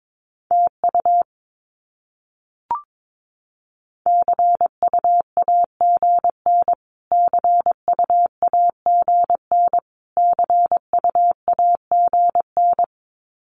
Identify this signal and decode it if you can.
Morse code, heard as a keyed tone, about 22 wpm, 700 Hz. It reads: TU CUAGN CUAGN CUAGN
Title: Abbreviations Repeated 3x - 22wpm